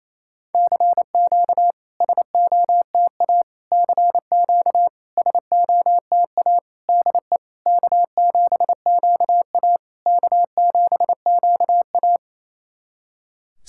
Morse code, heard as a keyed tone, about 29 wpm, 690 Hz. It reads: CQ SOTA CQ SOTA DE K7QA K7QA
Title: SOTA QSO Practice 28wpm